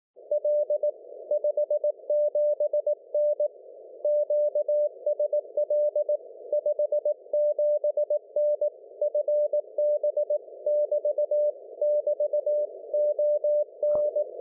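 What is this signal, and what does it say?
Morse code, heard as a keyed tone, about 18 wpm, 590 Hz. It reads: L 57N QSL 57N FB <BT> <BT> ON
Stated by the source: CW from G4GTW/M on 1942 kHz in the 160 m band.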